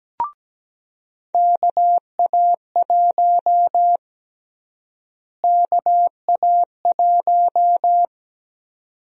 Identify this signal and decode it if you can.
Morse code, heard as a keyed tone, about 17 wpm, 700 Hz. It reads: KA1 KA1
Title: USA Call Sign Prefixes - Repeated 2x - 17wpm